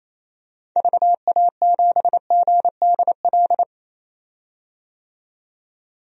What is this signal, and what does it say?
Morse code, heard as a keyed tone, about 27 wpm, 700 Hz. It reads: VA7GDL
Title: DXCC-100 Call Signs - 6 Characters - 28wpm